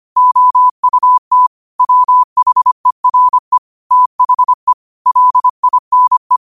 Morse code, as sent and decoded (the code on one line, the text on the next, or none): --- ..- - / .-- .... .-. / - .... / .-.. .. -. .
OUT WHERE THE LINE